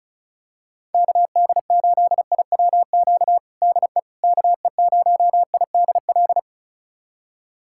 Morse code, heard as a keyed tone, about 36 wpm, 700 Hz. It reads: KD8IWQ DE KE0IDL
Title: USA Callsign DE Callsign - 6 Characters - 35wpm